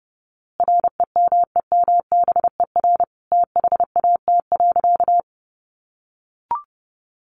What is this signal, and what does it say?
Morse code, heard as a keyed tone, about 30 wpm, 700 Hz. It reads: REMEMBER THAT.
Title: A Sound Of Thunder 30wpm